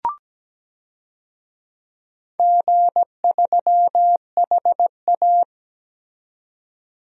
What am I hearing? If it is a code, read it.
Morse code, heard as a keyed tone, about 17 wpm, 700 Hz. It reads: G3HA
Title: DXCC-100 Call Signs - 4 Characters - 17wpm